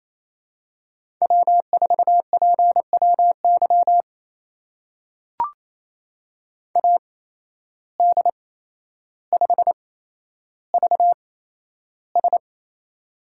Morse code, heard as a keyed tone, about 28 wpm, spaced as for 12 wpm, 700 Hz. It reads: W4PWY AD5VS
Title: USA Call Signs - 5 Characters - 8x Character Spacing - 28wpm